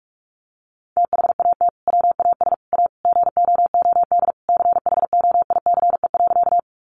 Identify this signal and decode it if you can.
Morse code, heard as a keyed tone, about 44 wpm, 700 Hz. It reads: THAT WAS A GOOD CHOICE.
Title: Sentences from Top 900 Words 45wpm